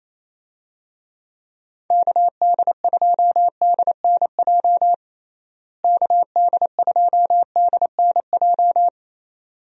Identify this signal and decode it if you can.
Morse code, heard as a keyed tone, about 27 wpm, 700 Hz. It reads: KD2DNJ KD2DNJ
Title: USA Call Signs - 6 Characters - Repeated 2x - 28wpm